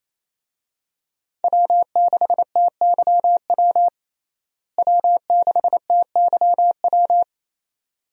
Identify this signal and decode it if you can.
Morse code, heard as a keyed tone, about 28 wpm, 700 Hz. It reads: W6TYW W6TYW
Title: USA Call Signs - 5 Characters - Repeated 2x - 28wpm